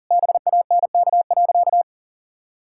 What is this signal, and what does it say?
Morse code, heard as a keyed tone, about 40 wpm, 700 Hz. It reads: BANK.
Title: Sentences from Top 1000 Words - Review Part 6 - 40wpm